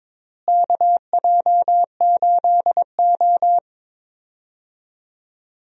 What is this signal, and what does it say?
Morse code, heard as a keyed tone, about 22 wpm, 700 Hz. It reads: KJ8O
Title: USA Call Signs - 4 Characters - Repeated 2x - 22wpm